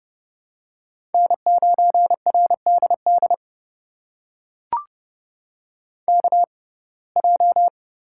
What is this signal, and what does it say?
Morse code, heard as a keyed tone, about 30 wpm, spaced as for 22 wpm, 700 Hz. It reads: N9RDD KJ
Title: USA Call Signs - 5 Characters - 6x Character Spacing - 30wpm